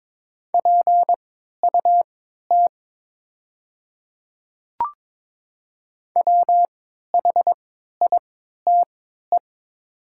Morse code, heard as a keyed tone, about 22 wpm, spaced as for 12 wpm, 700 Hz. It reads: PUT WHITE
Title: 3x Character Spacing - Top 500 Words - 22wpm